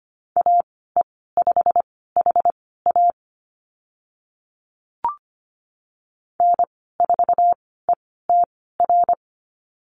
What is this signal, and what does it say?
Morse code, heard as a keyed tone, about 25 wpm, spaced as for 16 wpm, 700 Hz. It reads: AE5HA N4ETR
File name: Lesson 15g - Number 5 - Call Signs 25wpm